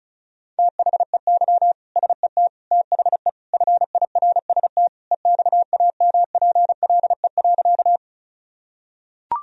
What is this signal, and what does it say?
Morse code, heard as a keyed tone, about 36 wpm, 700 Hz. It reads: THEY SET THE FIRST EXAMPLE.
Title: Sentences from Top 1000 Words - Review Part 1 - 35wpm